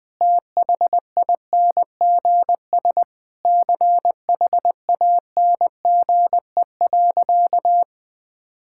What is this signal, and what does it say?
Morse code, heard as a keyed tone, about 21 wpm, 700 Hz. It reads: THINGS CHANGE.
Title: Sentences with 6-Word Phrases - 20wpm